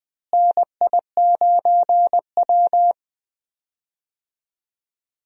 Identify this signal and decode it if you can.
Morse code, heard as a keyed tone, about 21 wpm, 700 Hz. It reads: NI9W
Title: USA Call Signs - 4 Characters - Repeated 3x - 20wpm